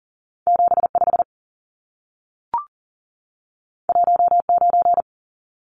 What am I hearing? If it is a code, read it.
Morse code, heard as a keyed tone, about 40 wpm, 700 Hz. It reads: 75 19
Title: Numbers - 2-Digit 40wpm